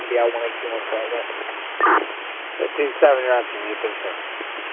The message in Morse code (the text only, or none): none